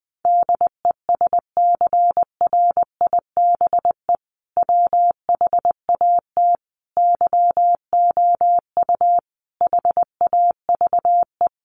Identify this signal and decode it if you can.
Morse code, heard as a keyed tone, about 20 wpm, 700 Hz. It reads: DESCRIBE WHAT YOU HAVE
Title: Sentences from Top 1000 Words 20wpm